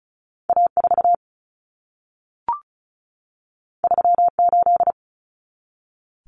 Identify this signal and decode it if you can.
Morse code, heard as a keyed tone, about 36 wpm, 700 Hz. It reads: A4 38
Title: Common Cut Numbers - 2-Digit 35wpm